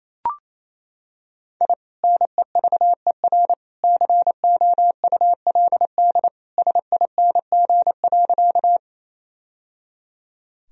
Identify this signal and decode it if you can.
Morse code, heard as a keyed tone, about 28 wpm, 700 Hz. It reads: I NEVER COULD SING.
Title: Sentences from Top 1000 Words - Review Part 2 - 28wpm